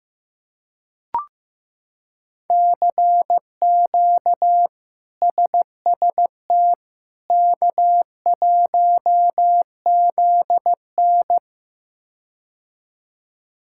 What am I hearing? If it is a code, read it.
Morse code, heard as a keyed tone, about 16 wpm, 700 Hz. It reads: CQ SST K1ZN
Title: K1USN SST QSO Practice - 15wpm